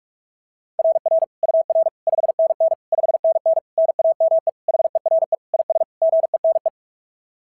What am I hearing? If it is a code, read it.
Morse code, heard as a keyed tone, about 46 wpm, 690 Hz. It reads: RR UR 5NN 5NN NAME HERE IS GENE
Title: SOTA QSO Practice 45wpm